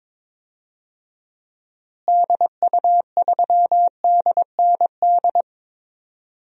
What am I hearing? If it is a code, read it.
Morse code, heard as a keyed tone, about 22 wpm, 700 Hz. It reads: DU3DND
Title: DXCC-100 Call Signs - 6 Characters - Repeated 2x - 22wpm